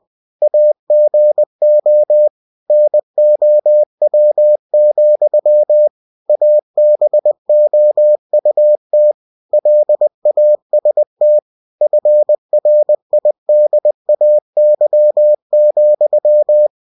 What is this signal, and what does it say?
Morse code, heard as a keyed tone, about 20 wpm, 600 Hz. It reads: AGO NOW, ABOUT LAST FRIDAY,